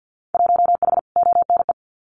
Morse code, heard as a keyed tone, about 50 wpm, 700 Hz. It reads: 1H ONE
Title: Winter Field Day Exchanges 50wpm